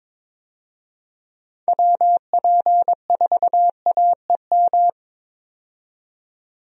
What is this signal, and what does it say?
Morse code, heard as a keyed tone, about 21 wpm, 700 Hz. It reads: WP4AEM
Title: USA Call Signs - 6 Characters - 22wpm